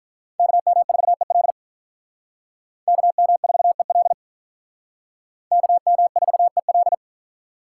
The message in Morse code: KM4EL KM4EL KM4EL

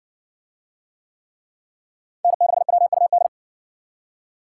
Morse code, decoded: N6WUD

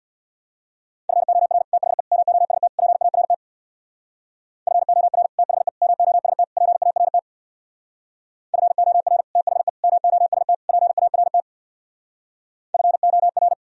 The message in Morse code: FOR THE MOST PART FOR THE MOST PART FOR THE MOST PART FOR

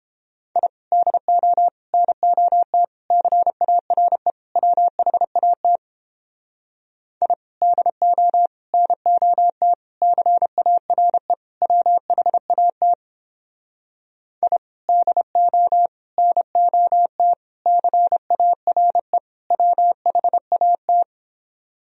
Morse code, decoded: I DO NOT CARE WHAT I DO NOT CARE WHAT I DO NOT CARE WHAT

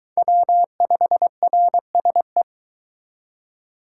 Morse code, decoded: W5RSE

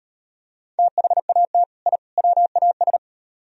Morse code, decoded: THAT I WAS

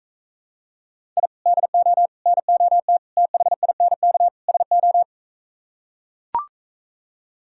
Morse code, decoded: I DO NOT THINK SO

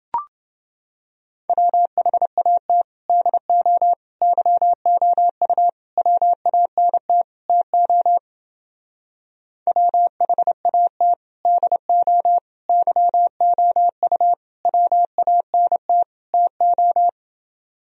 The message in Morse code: WHAT DO YOU WANT TO WHAT DO YOU WANT TO

